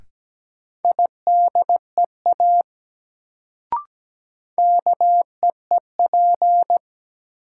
Morse code, heard as a keyed tone, about 17 wpm, 700 Hz. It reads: IDEA KEEP
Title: Sentences from Top 300 Words - New Word Drill - 17wpm